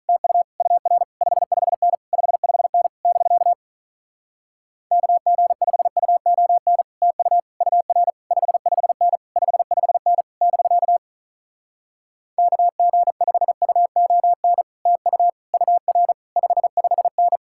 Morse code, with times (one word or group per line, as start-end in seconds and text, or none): 0.1-0.4 TU
0.6-1.0 UR
1.2-1.9 55N
2.1-2.9 55N
3.0-3.5 <BK>
4.9-6.8 KG5UON
7.0-7.4 TU
7.6-8.1 UR
8.3-9.2 55N
9.4-10.2 55N
10.4-11.0 <BK>
12.4-14.6 KG5UON
14.9-15.3 TU
15.5-16.1 UR
16.4-17.4 55N